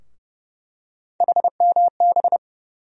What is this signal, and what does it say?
Morse code, heard as a keyed tone, about 30 wpm, 700 Hz. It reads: HMB
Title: USA Call Sign Postfixes - 3 Letters 30wpm